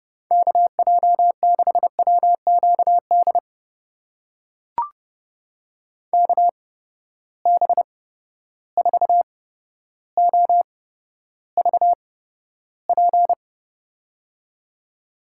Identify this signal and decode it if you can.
Morse code, heard as a keyed tone, about 30 wpm, spaced as for 13 wpm, 700 Hz. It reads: KJ6WQD KB4OVP
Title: USA Call Signs - 6 Characters - 8x Character Spacing - 30wpm